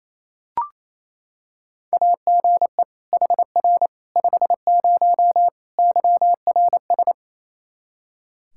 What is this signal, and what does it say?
Morse code, heard as a keyed tone, about 28 wpm, 700 Hz. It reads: AGE HR 50 YRS
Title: K1USN SST QSO Practice 28wpm